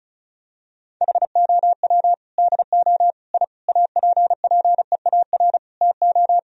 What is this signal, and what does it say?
Morse code, heard as a keyed tone, about 36 wpm, 700 Hz. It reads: HOW DO I APPEAR TO
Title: Sentences from Top 1000 Words - Review Part 2 - 35wpm